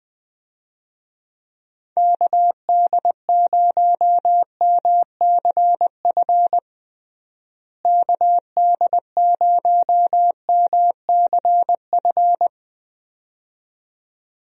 Morse code, heard as a keyed tone, about 21 wpm, 700 Hz. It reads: KD0MCF KD0MCF
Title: USA Call Signs - 6 Characters - Repeated 2x - 20wpm